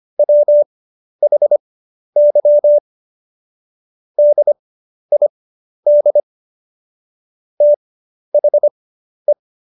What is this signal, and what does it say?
Morse code, heard as a keyed tone, about 25 wpm, spaced as for 11 wpm, 590 Hz. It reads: WHY DID THE
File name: Jokes-25-char-10-eff